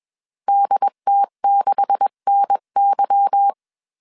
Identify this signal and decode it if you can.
Morse code, heard as a keyed tone, about 21 wpm, 800 Hz. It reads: DT6NY
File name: CW / DET / 1 9 1500C S B 20wpm 2313